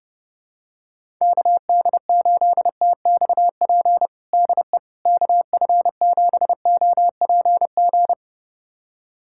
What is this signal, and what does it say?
Morse code, heard as a keyed tone, about 30 wpm, 700 Hz. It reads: KD8TXP DE KF7OPG
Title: USA Callsign DE Callsign - 6 Characters - 30wpm